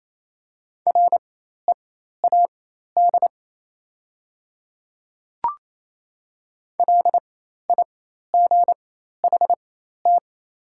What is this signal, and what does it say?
Morse code, heard as a keyed tone, about 28 wpm, spaced as for 13 wpm, 690 Hz. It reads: READ LIGHT
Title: 4x Character Spacing - Top 200 Words - 28wpm